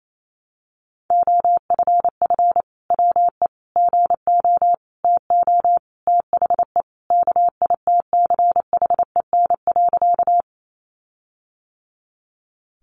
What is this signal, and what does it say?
Morse code, heard as a keyed tone, about 28 wpm, 700 Hz. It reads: OFF WE GO TO THE KITCHEN.
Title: Sentences from Top 1400 Words 28wpm